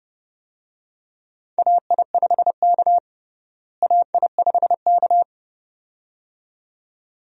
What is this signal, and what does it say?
Morse code, heard as a keyed tone, about 30 wpm, 700 Hz. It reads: AI5K AI5K
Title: USA Call Signs - 4 Characters - Repeated 2x - 30wpm